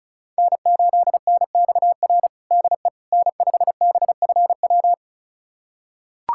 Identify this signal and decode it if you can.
Morse code, heard as a keyed tone, about 36 wpm, 700 Hz. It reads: N8NXR DE N5BFW
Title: USA Callsign DE Callsign - 5 Characters - 35wpm